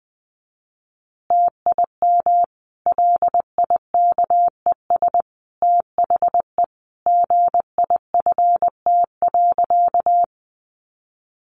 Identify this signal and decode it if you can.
Morse code, heard as a keyed tone, about 20 wpm, 700 Hz. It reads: TIM LIKES THE GIFT.
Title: Easy Phrases 20wpm